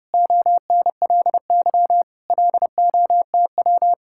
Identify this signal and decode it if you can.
Morse code, heard as a keyed tone, about 33 wpm, 700 Hz. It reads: ONLY LOTW